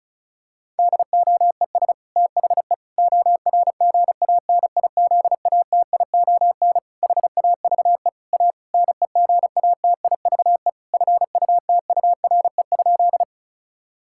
Morse code, heard as a keyed tone, about 34 wpm, 700 Hz. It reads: DOES THE ORGANIZATION HAVE A NEGATIVE FUTURE?
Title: Sentences from Top 1000 Words - Review Part 6 - 35wpm